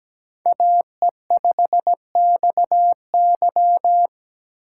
Morse code, heard as a keyed tone, about 17 wpm, 700 Hz. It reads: AE5XY
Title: USA Call Signs - 5 Characters - 2x Character Spacing - 17wpm